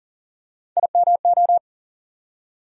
Morse code, heard as a keyed tone, about 40 wpm, 700 Hz. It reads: IMO
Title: USA Call Sign Postfixes - 3 Letters 40wpm